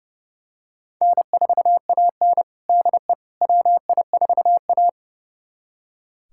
Morse code, heard as a keyed tone, about 32 wpm, 700 Hz. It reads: N4AN DE WI4A
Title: USA Callsign DE Callsign - 4 Characters - 30wpm